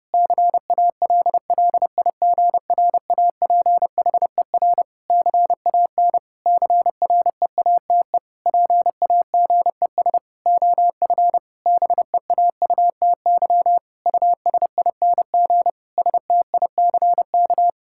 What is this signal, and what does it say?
Morse code, heard as a keyed tone, about 30 wpm, 700 Hz. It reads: CALLIGRAPHER CAN CREATE PAGES OF BEAUTY USING STICK
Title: Pangram Challenge v2 - 30wpm